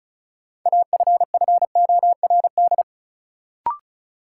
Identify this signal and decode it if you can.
Morse code, heard as a keyed tone, about 36 wpm, 700 Hz. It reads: AFFORD
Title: Sentences from Top 1100 Words - New Word Drill Alt - 35wpm